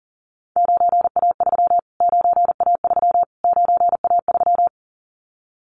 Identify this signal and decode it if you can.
Morse code, heard as a keyed tone, about 40 wpm, 700 Hz. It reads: 9A3 9A3 9A3
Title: Callsign Prefixes for Top 25 DXCC - Repeated 3x - 40wpm